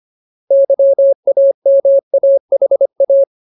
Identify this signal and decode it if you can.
Morse code, heard as a keyed tone, about 25 wpm, 550 Hz. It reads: YAMAHA
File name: Ultimate Word List / 25WPM Ultimate